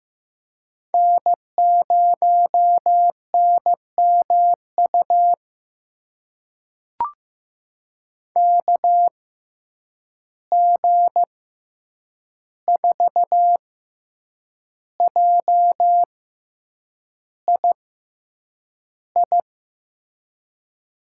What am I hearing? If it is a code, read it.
Morse code, heard as a keyed tone, about 15 wpm, spaced as for 7 wpm, 700 Hz. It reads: N0NMU KG4JII
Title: USA Call Signs - 6x Character Spacing - 15wpm